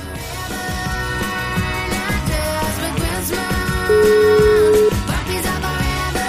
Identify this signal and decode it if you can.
Ringback tone (Europe, 425 Hz), with noise.